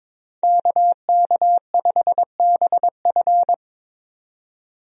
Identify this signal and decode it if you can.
Morse code, heard as a keyed tone, about 22 wpm, 700 Hz. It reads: KK5BF
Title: USA Call Signs - 5 Characters - 22wpm